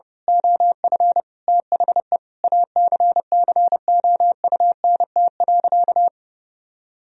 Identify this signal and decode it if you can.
Morse code, heard as a keyed tone, about 32 wpm, 700 Hz. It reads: OF THE ACCOUNT.
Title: Sentences from Top 1000 Words - Review Part 4 - 30wpm